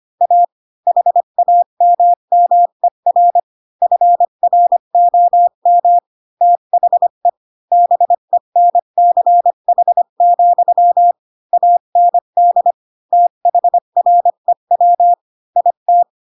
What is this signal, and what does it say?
Morse code, heard as a keyed tone, about 25 wpm, 700 Hz. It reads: A HAMMER FROM THE BENCH, AND THREW IT
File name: Pinocchio / Pinocchio ch4 20 25 ik1hge